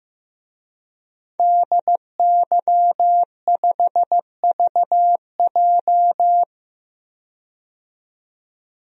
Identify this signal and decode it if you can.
Morse code, heard as a keyed tone, about 15 wpm, 700 Hz. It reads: DY5VJ
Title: DXCC-100 Call Signs - 5 Characters - 15wpm